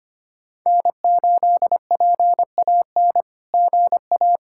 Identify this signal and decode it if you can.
Morse code, heard as a keyed tone, about 25 wpm, 700 Hz. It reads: N8PAN GA
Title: SOTA QSO Practice 25wpm